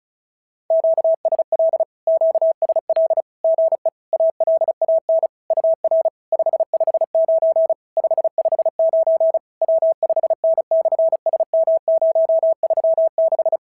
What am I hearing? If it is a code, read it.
Morse code, heard as a keyed tone, about 36 wpm, 650 Hz. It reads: QSL QSL GE ALAN UR 559 559 W5N/SM036
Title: SOTA QSO Practice 35wpm